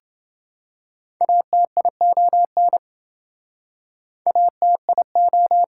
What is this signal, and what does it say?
Morse code, heard as a keyed tone, about 28 wpm, 700 Hz. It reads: ATION ATIO